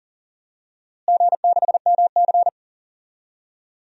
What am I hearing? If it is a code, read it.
Morse code, heard as a keyed tone, about 40 wpm, 700 Hz. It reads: G6MC
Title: DXCC-100 Call Signs - 4 Characters - Repeated 2x - 40wpm